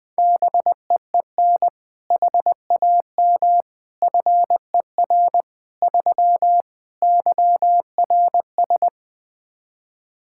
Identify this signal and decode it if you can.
Morse code, heard as a keyed tone, about 21 wpm, 700 Hz. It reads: BEEN HAM FER 3 YRS
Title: K1USN SST QSO Practice 20wpm